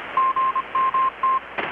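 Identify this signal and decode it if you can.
Morse code, heard as a keyed tone, about 25 wpm, 1060 Hz, with radio noise. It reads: GMT
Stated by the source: Morse from KPH's Night of Nights XX maritime radiotelegraphy event, received in Chiapas, Mexico about 3,800 km away on a Kenwood R-600 with a grounded 10 m long-wire antenna.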